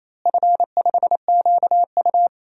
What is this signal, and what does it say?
Morse code, heard as a keyed tone, about 28 wpm, 700 Hz. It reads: F5QU
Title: DXCC-100 Call Signs - 4 Characters - Repeated 3x - 28wpm